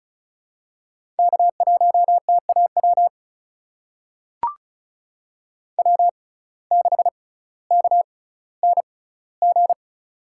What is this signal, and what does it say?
Morse code, heard as a keyed tone, about 36 wpm, spaced as for 18 wpm, 700 Hz. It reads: K1TAW W6KNG
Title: USA Call Signs - 5 Characters - 6x Character Spacing - 35wpm